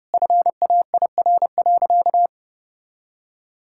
Morse code, heard as a keyed tone, about 30 wpm, 700 Hz. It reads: FAIR.
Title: Sentences from Top 1600 Words 30wpm